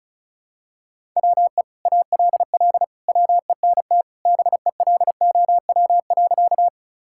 Morse code, heard as a keyed tone, about 36 wpm, 700 Hz. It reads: WE ALL WENT BELOW.
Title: Sentences from Top 1000 Words - Review Part 5 - 35wpm